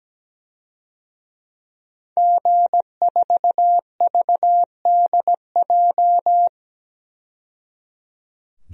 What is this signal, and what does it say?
Morse code, heard as a keyed tone, about 17 wpm, 700 Hz. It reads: G4VDJ
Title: DXCC-100 Call Signs - 5 Characters - 17wpm